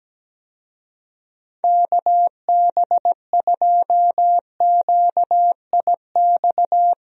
Morse code, heard as a keyed tone, about 17 wpm, 700 Hz. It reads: KB2QIX
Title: USA Call Signs - 6 Characters - Repeated 3x - 17wpm